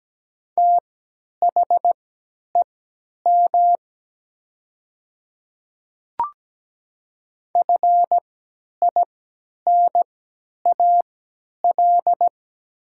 Morse code, heard as a keyed tone, about 17 wpm, spaced as for 10 wpm, 700 Hz. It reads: THEM FINAL
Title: 3x Character Spacing - Top 500 Words - 17wpm